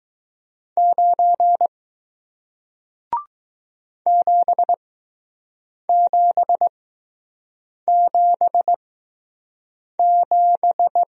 Morse code, heard as a keyed tone, about 20 wpm, spaced as for 6 wpm, 700 Hz. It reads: 9 7777